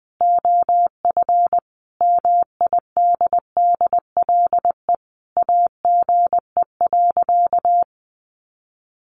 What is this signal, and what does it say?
Morse code, heard as a keyed tone, about 20 wpm, 700 Hz. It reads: OF MIDDLE AGE.
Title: Sentences from Top 1400 Words 20wpm